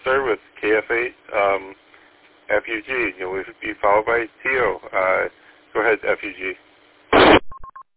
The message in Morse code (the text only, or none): none